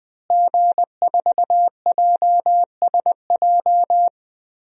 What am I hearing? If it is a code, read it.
Morse code, heard as a keyed tone, about 20 wpm, 700 Hz. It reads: G4JSJ
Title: DXCC-100 Call Signs - 5 Characters - Repeated 2x - 20wpm